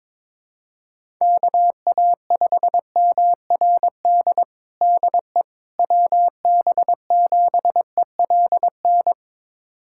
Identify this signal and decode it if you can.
Morse code, heard as a keyed tone, about 23 wpm, 700 Hz. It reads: KA5MRD DE WB7ELN
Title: USA Callsign DE Callsign 22wpm